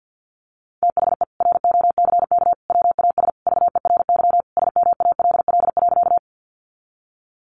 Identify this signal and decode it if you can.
Morse code, heard as a keyed tone, about 50 wpm, 700 Hz. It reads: THE ROCK WAS VERY SMALL.
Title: Sentences from Top 400 Words 50wpm